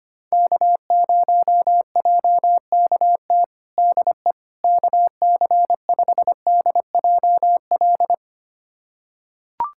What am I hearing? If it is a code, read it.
Morse code, heard as a keyed tone, about 25 wpm, 700 Hz. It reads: K0JKT DE KC5DJL
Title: USA Callsign DE Callsign 25wpm